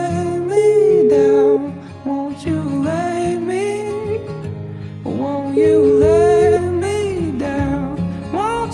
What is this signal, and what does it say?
Ringback tone (Europe, 425 Hz), with noise.